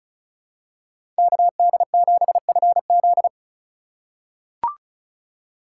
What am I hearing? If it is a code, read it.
Morse code, heard as a keyed tone, about 36 wpm, 700 Hz. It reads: KD7FZ
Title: USA Call Signs - 5 Characters - 6x Character Spacing - 35wpm